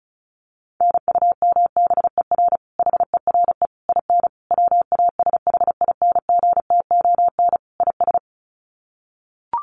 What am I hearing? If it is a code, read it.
Morse code, heard as a keyed tone, about 34 wpm, 700 Hz. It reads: NUMBER HERE IN WASHINGTON IS
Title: Sets of 5 Words Encore2 35wpm